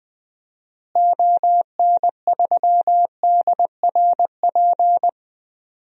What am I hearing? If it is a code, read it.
Morse code, heard as a keyed tone, about 20 wpm, 700 Hz. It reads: ON3DRP
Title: DXCC-100 Call Signs - 6 Characters - Repeated 2x - 20wpm